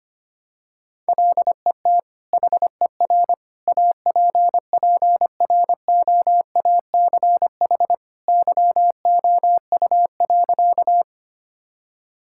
Morse code, heard as a keyed tone, about 25 wpm, 700 Hz. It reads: LET HER APPROACH YOU.
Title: Sentences from Top 1000 Words - Review Part 6 - 25wpm